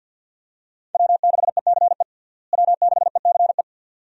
Morse code, heard as a keyed tone, about 52 wpm, 700 Hz. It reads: W6ECE W6ECE
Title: USA Call Signs - 5 Characters - Repeated 2x - 50wpm